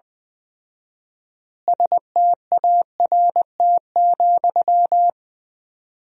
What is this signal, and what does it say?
Morse code, heard as a keyed tone, about 20 wpm, 700 Hz. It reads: START,
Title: 4x Word-Spacing - Sentences from Top 300 Words - 20wpm